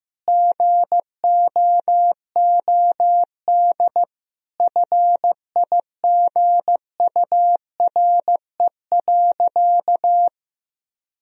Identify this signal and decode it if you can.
Morse code, heard as a keyed tone, about 16 wpm, 700 Hz. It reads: GOOD FIGURE.
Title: Sentences from Top 1000 Words - Review Part 2 - 15wpm